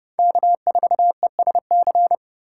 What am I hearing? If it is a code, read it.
Morse code, heard as a keyed tone, about 30 wpm, 700 Hz. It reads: K4ESC
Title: USA Call Signs - 5 Characters - 3x Character Spacing - 30wpm